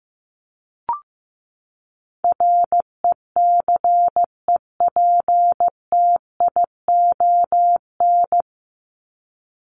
Morse code, heard as a keyed tone, about 15 wpm, 700 Hz. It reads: RECEPTION
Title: Top 1751-2000 Words 15wpm